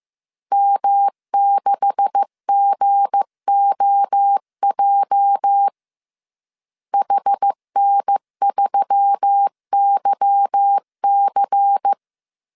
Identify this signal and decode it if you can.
Morse code, heard as a keyed tone, about 15 wpm, 800 Hz. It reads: M6GOJ HN3YC
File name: CW / DET / 1 9 1500C S A 14wpm 3222